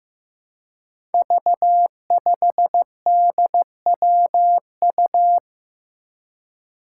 Morse code, heard as a keyed tone, about 15 wpm, 700 Hz. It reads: V5DWU
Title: Five Challenging Letter-Number Combinations 15wpm